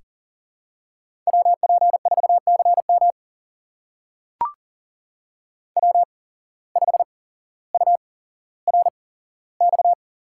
Morse code, heard as a keyed tone, about 40 wpm, spaced as for 17 wpm, 700 Hz. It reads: WP4CM W5URX
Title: USA Call Signs - 8x Character Spacing - 40wpm